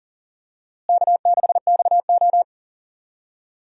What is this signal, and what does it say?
Morse code, heard as a keyed tone, about 40 wpm, 700 Hz. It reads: K6XO
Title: USA Call Signs - 4 Characters - Repeated 3x - 40wpm